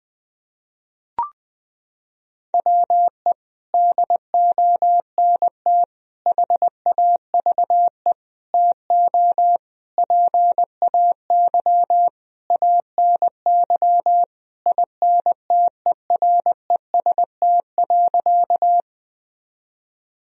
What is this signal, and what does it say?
Morse code, heard as a keyed tone, about 20 wpm, 700 Hz. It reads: WE DONT HAVE TO PAY ANY INTEREST.
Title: Sentences from Top 1000 Words - Review Part 2 - 20wpm